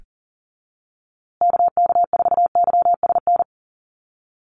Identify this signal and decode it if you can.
Morse code, heard as a keyed tone, about 40 wpm, 700 Hz. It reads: KK4YSN
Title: USA Call Signs Repeated 2x 40wpm